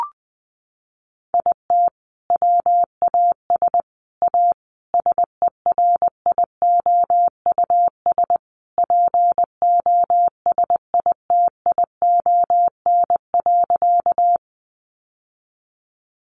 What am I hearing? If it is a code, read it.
Morse code, heard as a keyed tone, about 20 wpm, 700 Hz. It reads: IT WAS A SERIOUS POSITION.
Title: Sentences from Top 1100 Words 20wpm